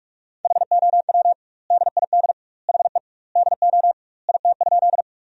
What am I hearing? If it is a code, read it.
Morse code, heard as a keyed tone, about 46 wpm, 700 Hz. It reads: HOW DID HE DO IT?
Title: 2x Word-Spacing - Sentences from Top 100 Words - 45wpm